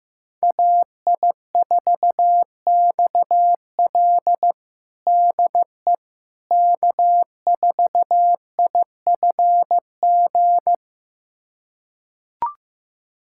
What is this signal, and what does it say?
Morse code, heard as a keyed tone, about 15 wpm, 700 Hz. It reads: AI4XL DE K4IFG
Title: USA Callsign DE Callsign - 5 Characters - 15wpm